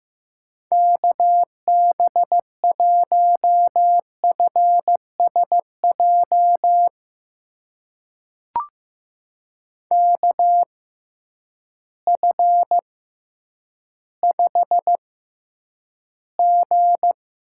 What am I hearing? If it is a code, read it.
Morse code, heard as a keyed tone, about 15 wpm, spaced as for 9 wpm, 700 Hz. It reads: KB1FSJ KF5G
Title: USA Call Signs - 6 Characters - 6x Character Spacing - 15wpm